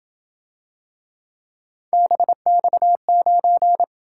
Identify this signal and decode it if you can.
Morse code, heard as a keyed tone, about 26 wpm, 700 Hz. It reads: BX9